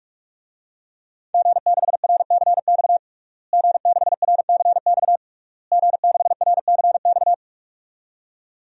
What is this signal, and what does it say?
Morse code, heard as a keyed tone, about 46 wpm, 700 Hz. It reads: G6RCX G6RCX G6RCX
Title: DXCC-100 Call Signs - 5 Characters - Repeated 3x - 45wpm